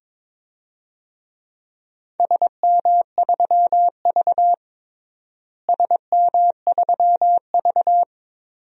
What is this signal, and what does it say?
Morse code, heard as a keyed tone, about 22 wpm, 700 Hz. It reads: SM3V SM3V
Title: DXCC-100 Call Signs - 4 Characters - Repeated 2x - 22wpm